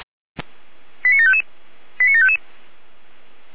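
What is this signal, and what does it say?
Five-tone selective call (ZVEI), heard as quick runs of five tones, tones of about 80 ms. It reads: 89755 89755
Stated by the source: ZVEI five-tone selective call received by SDR.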